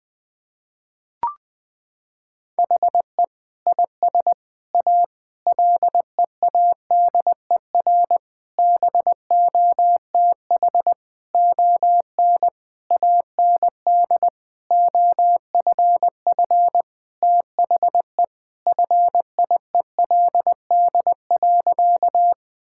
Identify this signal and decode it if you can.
Morse code, heard as a keyed tone, about 20 wpm, 700 Hz. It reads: HE IS A LEADER BOTH ON AND OFF THE FIELD.
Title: Sentences with 6-Word Phrases - 20wpm